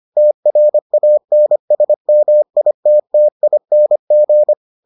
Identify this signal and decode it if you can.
Morse code, heard as a keyed tone, about 25 wpm, 600 Hz. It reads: TRANSMITTING